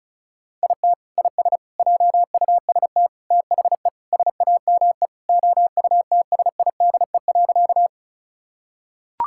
Morse code, heard as a keyed tone, about 36 wpm, 690 Hz. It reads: IT IS JUST THE SAME OUTSIDE.
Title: Sentences from Top 1000 Words - Review Part 6 - 35wpm